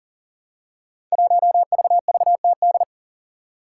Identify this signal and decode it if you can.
Morse code, heard as a keyed tone, about 40 wpm, 700 Hz. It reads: 1VVTD
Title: Five Challenging Letter-Number Combinations 40wpm